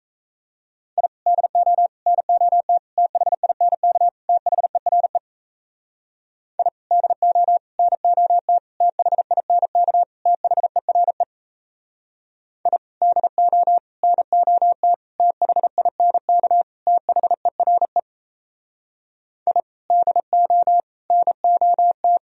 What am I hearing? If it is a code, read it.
Morse code, heard as a keyed tone, about 35 wpm, 700 Hz. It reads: I DO NOT THINK THERE I DO NOT THINK THERE I DO NOT THINK THERE I DO NOT
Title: Speed-Racing - Sets of 5 Words 28wpm